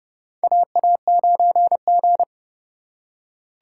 Morse code, heard as a keyed tone, about 30 wpm, 700 Hz. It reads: AA9G
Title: USA Call Signs - 4 Characters - 3x Character Spacing - 30wpm